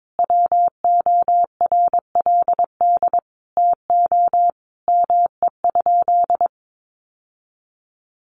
Morse code, heard as a keyed tone, about 22 wpm, 700 Hz. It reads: WORLD TO ME?
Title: Sentences from Top 900 Words 22wpm